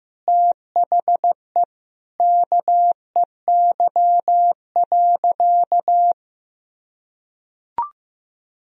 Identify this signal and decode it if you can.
Morse code, heard as a keyed tone, about 15 wpm, 700 Hz. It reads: THE KEY.
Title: Sentences from Top 1000 Words - Review Part 4 - 15wpm